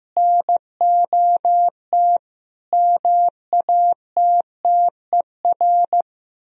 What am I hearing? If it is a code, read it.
Morse code, heard as a keyed tone, about 16 wpm, 700 Hz. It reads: NOT MATTER
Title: Speed-Racing - Sets of 4 Words 15wpm